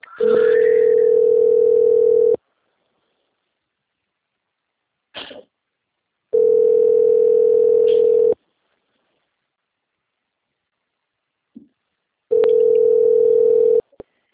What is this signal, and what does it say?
Ringback tone (North America), 440 + 480 Hz.